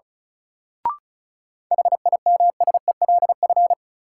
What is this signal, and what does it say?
Morse code, heard as a keyed tone, about 36 wpm, 700 Hz. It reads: HIMSELF
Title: Sentences from Top 1500 Words - New Word Drill - 35wpm